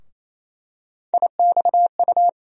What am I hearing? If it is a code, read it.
Morse code, heard as a keyed tone, about 28 wpm, 700 Hz. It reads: IXU
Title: USA Call Sign Postfixes - 3 Letters 28wpm